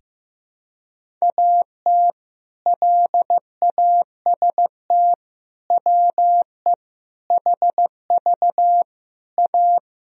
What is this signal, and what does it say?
Morse code, heard as a keyed tone, about 15 wpm, 700 Hz. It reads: AT LAST WE HV A
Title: Sentences from Top 300 Words and Abbreviations 15wpm